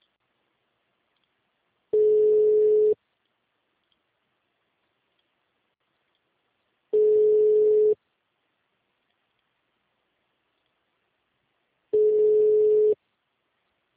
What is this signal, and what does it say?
Ringback tone (Europe, 425 Hz).
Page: Capital - SP